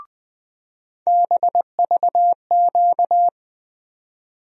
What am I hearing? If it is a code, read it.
Morse code, heard as a keyed tone, about 20 wpm, 700 Hz. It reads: BVQ
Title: USA Call Sign Postfixes - 3 Letters 20wpm